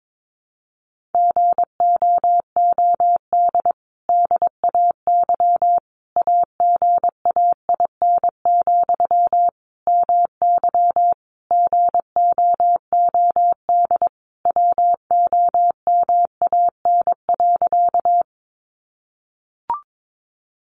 Morse code, heard as a keyed tone, about 22 wpm, 700 Hz. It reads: GOOD DAY AGAIN, MY GOOD WOMAN.
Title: Sentences from Top 1100 Words 22wpm